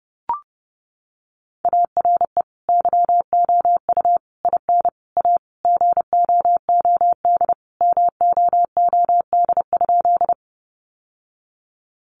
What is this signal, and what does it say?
Morse code, heard as a keyed tone, about 30 wpm, 700 Hz. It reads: ARE YOU IN A GOOD MOOD?
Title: Sentences from Top 1100 Words 30wpm